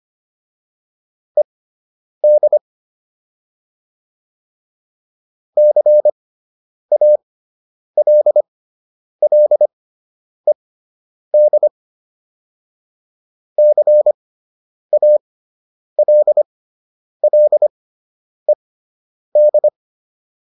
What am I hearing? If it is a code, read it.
Morse code, heard as a keyed tone, about 25 wpm, spaced as for 9 wpm, 600 Hz. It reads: ED CALLED CALLED